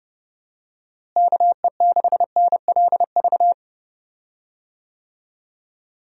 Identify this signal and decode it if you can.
Morse code, heard as a keyed tone, about 31 wpm, 700 Hz. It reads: KE6NLV
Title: USA Call Signs - 6 Characters - Repeated 3x - 30wpm